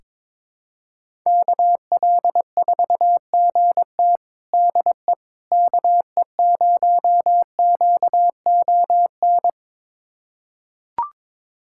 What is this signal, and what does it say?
Morse code, heard as a keyed tone, about 22 wpm, 700 Hz. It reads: KL4GT DE KE0QON
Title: USA Callsign DE Callsign 22wpm